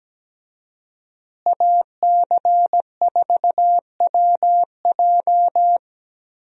Morse code, heard as a keyed tone, about 17 wpm, 700 Hz. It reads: AC4WJ
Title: USA Call Signs - 5 Characters - Repeated 2x - 17wpm